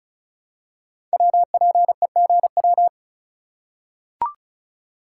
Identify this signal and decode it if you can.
Morse code, heard as a keyed tone, about 36 wpm, 700 Hz. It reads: WPEGW
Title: Code Groups - 5 Letter:Numbers - 35wpm